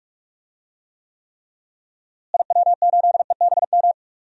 Morse code, heard as a keyed tone, about 44 wpm, 700 Hz. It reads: IW8EBM
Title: DXCC-100 Call Signs - 6 Characters - Repeated 3x - 45wpm